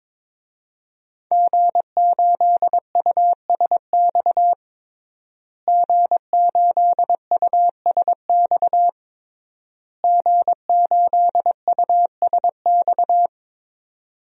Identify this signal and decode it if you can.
Morse code, heard as a keyed tone, about 22 wpm, 700 Hz. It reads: G8USX G8USX G8USX
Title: DXCC-100 Call Signs - 5 Characters - Repeated 3x - 22wpm